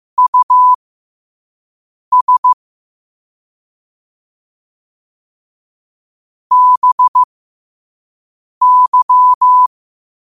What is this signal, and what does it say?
Morse code, heard as a keyed tone, about 15 wpm, spaced as for 5 wpm, 1000 Hz. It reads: US BY
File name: qotdfiles / 5wpm